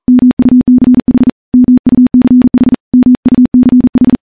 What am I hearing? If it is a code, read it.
Morse code, heard as a keyed tone, about 42 wpm, 250 Hz. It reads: MUCH MUCH MUCH
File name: morse-training-40WPM